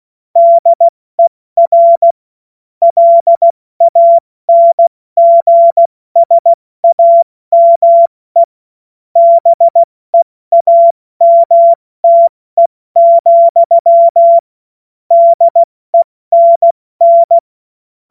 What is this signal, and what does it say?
Morse code, heard as a keyed tone, about 16 wpm, 680 Hz. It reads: DER LANGSAME BEAMTE, DENN
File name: Witz-21-Oktober-2025-70-F80